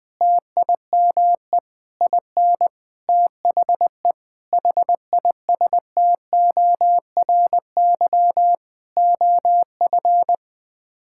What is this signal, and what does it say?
Morse code, heard as a keyed tone, about 20 wpm, 700 Hz. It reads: TIME IN THE HISTORY OF
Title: Sets of 8 Words - 20wpm